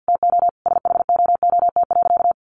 Morse code, heard as a keyed tone, about 50 wpm, 700 Hz. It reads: TO SHOOT.
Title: A Sound Of Thunder 50wpm